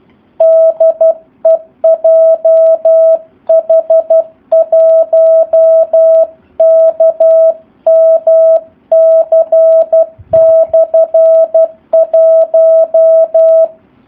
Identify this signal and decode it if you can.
Morse code, heard as a keyed tone, about 12 wpm, 650 Hz. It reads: DE JH1KMC/1
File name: jh8cbh / 2019cw / SP04- w